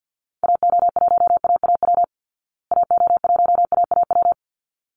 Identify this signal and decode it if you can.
Morse code, heard as a keyed tone, about 48 wpm, 700 Hz. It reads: AO1AAW AO1AAW
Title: Call Signs Repeated 3x - 50wpm